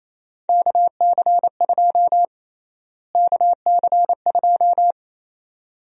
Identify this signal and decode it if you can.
Morse code, heard as a keyed tone, about 27 wpm, 700 Hz. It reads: KC2 KC2
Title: USA Call Sign Prefixes - Repeated 2x - 28wpm